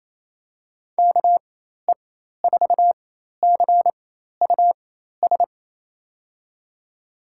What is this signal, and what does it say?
Morse code, heard as a keyed tone, about 28 wpm, spaced as for 13 wpm, 700 Hz. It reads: KE4CUS
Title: USA Call Signs - 4x Character Spacing - 28wpm